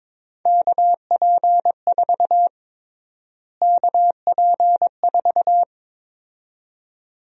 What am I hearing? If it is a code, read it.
Morse code, heard as a keyed tone, about 21 wpm, 700 Hz. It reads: KP4 KP4
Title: USA Call Sign Prefixes - Repeated 2x - 22wpm